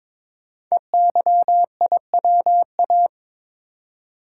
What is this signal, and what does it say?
Morse code, heard as a keyed tone, about 23 wpm, 700 Hz. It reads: EYIWA
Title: Code Groups - 5 Letters - 22wpm